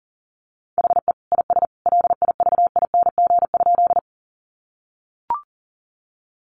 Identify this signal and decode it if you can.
Morse code, heard as a keyed tone, about 40 wpm, 700 Hz. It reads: HE IS LIVING?
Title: Sentences from Top 1100 Words 40wpm